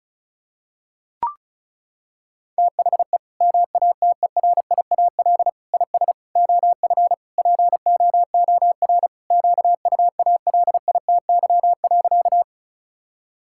Morse code, heard as a keyed tone, about 34 wpm, 700 Hz. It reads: THE MATERIAL IS OF POOR QUALITY.
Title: Sentences from Top 1000 Words - Review Part 5 - 35wpm